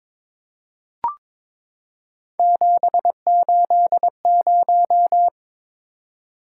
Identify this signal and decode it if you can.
Morse code, heard as a keyed tone, about 22 wpm, 700 Hz. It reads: 780
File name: practice / 22wpm / Three Letter-Number Combinations - 22wpm